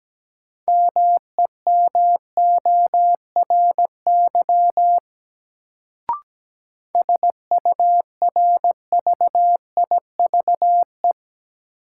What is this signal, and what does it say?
Morse code, heard as a keyed tone, about 17 wpm, 700 Hz. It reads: MEMORY SURVIVE
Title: Sentences from Top 1300 Words - New Word Drill Alt - 17wpm